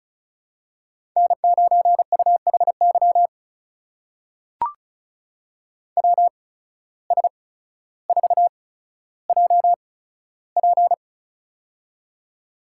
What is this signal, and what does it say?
Morse code, heard as a keyed tone, about 36 wpm, spaced as for 15 wpm, 700 Hz. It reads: N9UHY WS4JP
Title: USA Call Signs - 5 Characters - 8x Character Spacing - 35wpm